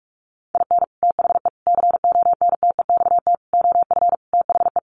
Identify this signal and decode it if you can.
Morse code, heard as a keyed tone, about 44 wpm, 700 Hz. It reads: IN THE CONTEXT OF THE
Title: Sets of 5 Words 45wpm